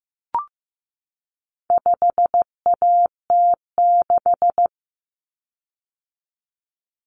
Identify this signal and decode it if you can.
Morse code, heard as a keyed tone, about 15 wpm, 700 Hz. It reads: 5AT6
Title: Common Cut Numbers - 4-Digit 15wpm